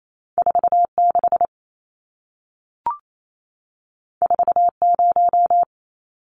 Morse code, heard as a keyed tone, about 28 wpm, 700 Hz. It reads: 46 40
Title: Numbers - 2-Digit 28wpm